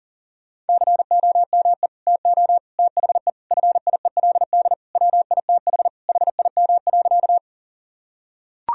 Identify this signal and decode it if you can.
Morse code, heard as a keyed tone, about 40 wpm, 700 Hz. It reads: COME TO THE FIELD WITH HIM.
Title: Sentences from Top 1000 Words - Review Part 2 - 40wpm